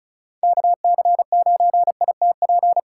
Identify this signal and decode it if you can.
Morse code, heard as a keyed tone, about 36 wpm, 700 Hz. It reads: KC9ITP
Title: USA Call Signs - 6 Characters - 4x Character Spacing - 35wpm